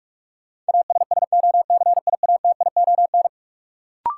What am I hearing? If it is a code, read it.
Morse code, heard as a keyed tone, about 44 wpm, 700 Hz. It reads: ASSOCIATION
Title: Sentences from Top 1100 Words - New Word Drill Alt - 45wpm